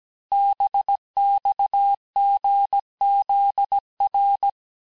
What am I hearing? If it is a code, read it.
Morse code, heard as a keyed tone, about 17 wpm, 800 Hz. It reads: BXGZR
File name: cours-du-lundi files / cours-1587-texte-chiffre-vitesse-16MPM